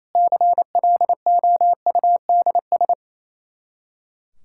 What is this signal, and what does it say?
Morse code, heard as a keyed tone, about 28 wpm, 700 Hz. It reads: CLOUDS
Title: Practice QSO - Exchange 2 - Info - 28wpm